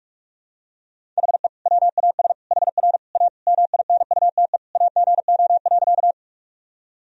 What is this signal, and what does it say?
Morse code, heard as a keyed tone, about 44 wpm, 700 Hz. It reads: HE WAS HR A MINUTE AGO.
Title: Sentences from Top 500 Words and Abbreviations 45wpm